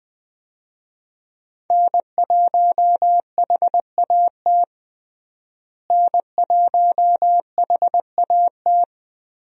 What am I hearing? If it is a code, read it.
Morse code, heard as a keyed tone, about 20 wpm, 700 Hz. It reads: N1HAT N1HAT
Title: USA Call Signs - 5 Characters - Repeated 2x - 20wpm